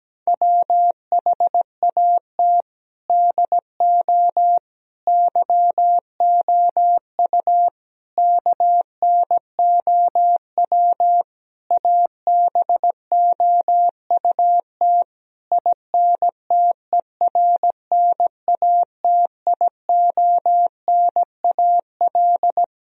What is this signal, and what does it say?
Morse code, heard as a keyed tone, about 17 wpm, 700 Hz. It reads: WHAT DO YOU KNOW ABOUT INTERNATIONAL
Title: Sentences from Top 1000 Words - Review Part 4 - 17wpm